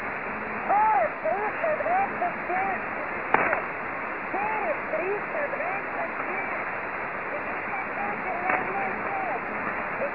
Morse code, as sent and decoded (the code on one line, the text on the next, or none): none